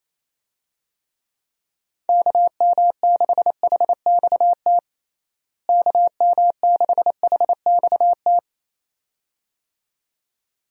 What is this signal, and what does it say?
Morse code, heard as a keyed tone, about 28 wpm, 700 Hz. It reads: KM6HXT KM6HXT
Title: USA Call Signs - 6 Characters - Repeated 2x - 28wpm